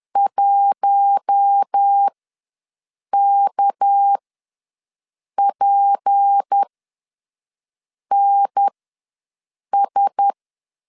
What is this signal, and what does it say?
Morse code, heard as a keyed tone, about 10 wpm, spaced as for 5 wpm, 800 Hz. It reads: 1KPNS
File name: LEZIONI / 7 GF9Z / 7 GF9Z ws 10wpm 3919